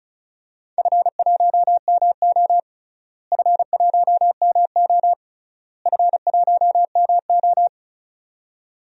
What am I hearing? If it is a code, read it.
Morse code, heard as a keyed tone, about 36 wpm, 700 Hz. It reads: F1MO F1MO F1MO
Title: DXCC-100 Call Signs - 4 Characters - Repeated 3x - 35wpm